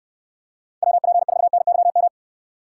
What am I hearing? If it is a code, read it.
Morse code, heard as a keyed tone, about 62 wpm, 700 Hz. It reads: WP4NYM